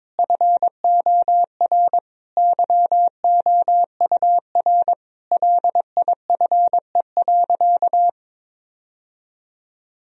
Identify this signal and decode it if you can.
Morse code, heard as a keyed tone, about 22 wpm, 700 Hz. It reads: FOR YOUR LIFE.
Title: Sentences from Top 1000 Words - Review Part 2 - 22wpm